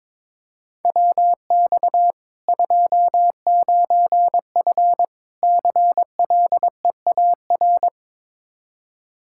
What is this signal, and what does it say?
Morse code, heard as a keyed tone, about 22 wpm, 700 Hz. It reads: WX 29F CLEAR
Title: Practice QSO - Exchange 2 - Info - 22wpm